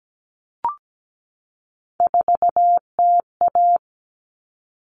Morse code, heard as a keyed tone, about 17 wpm, 700 Hz. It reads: 4TA
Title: All Cut Numbers - 3-Digit 17wpm